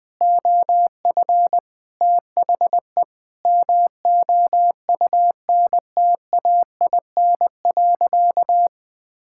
Morse code, heard as a keyed tone, about 20 wpm, 700 Hz. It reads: OF THE MOUNTAIN.
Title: Sentences with 5-Word Phrases - Spoken Beginning - 20wpm